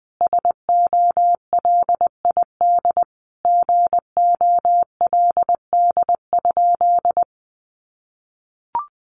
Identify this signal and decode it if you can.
Morse code, heard as a keyed tone, about 20 wpm, 700 Hz. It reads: SOLID GOLD?
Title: Sentences from Top 1500 Words 20wpm